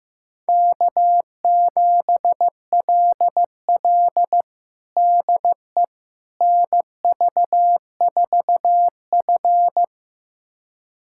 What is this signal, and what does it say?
Morse code, heard as a keyed tone, about 16 wpm, 700 Hz. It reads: K7LL DE NV4F
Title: USA Callsign DE Callsign - 4 Characters - 15wpm